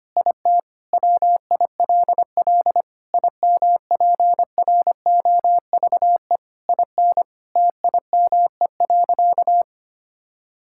Morse code, heard as a keyed tone, about 25 wpm, 700 Hz. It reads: IT WILL IMPROVE IN TIME.
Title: Sentences from Top 1000 Words - Review Part 5 - 25wpm